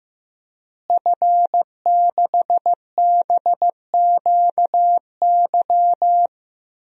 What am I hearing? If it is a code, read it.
Morse code, heard as a keyed tone, about 16 wpm, 700 Hz. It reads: F6BQY
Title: DXCC-100 Call Signs - 5 Characters - Repeated 3x - 15wpm